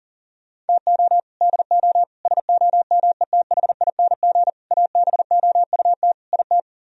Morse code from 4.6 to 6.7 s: ABOUT IT

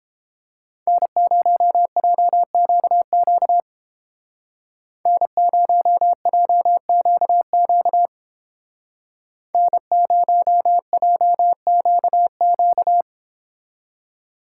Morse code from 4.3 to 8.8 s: N0JQQ

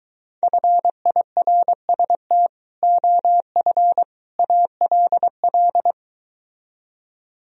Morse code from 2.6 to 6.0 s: OF ALL

